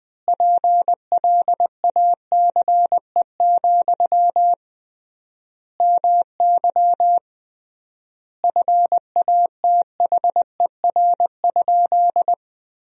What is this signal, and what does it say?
Morse code, heard as a keyed tone, about 20 wpm, 700 Hz. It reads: PLACE, MY FATHER?
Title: 3x Word-Spacing - Sentences from Top 400 Words - 20wpm